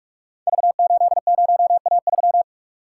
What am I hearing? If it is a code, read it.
Morse code, heard as a keyed tone, about 44 wpm, 700 Hz. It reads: V80A3
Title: Code Groups - 5 Letter:Numbers - 45wpm